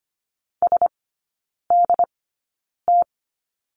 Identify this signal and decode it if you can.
Morse code, heard as a keyed tone, about 25 wpm, spaced as for 9 wpm, 700 Hz. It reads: SDT
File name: Lesson 14e - Number 2 - Three Characters 25wpm